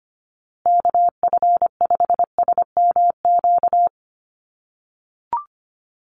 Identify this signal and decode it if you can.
Morse code, heard as a keyed tone, about 25 wpm, 700 Hz. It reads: KF5SMQ
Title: USA Call Signs Repeated 3x 25wpm